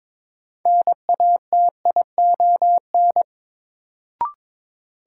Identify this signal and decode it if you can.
Morse code, heard as a keyed tone, about 22 wpm, 700 Hz. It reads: NATION
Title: Sentences from Top 900 Words - New Word Drill - 22wpm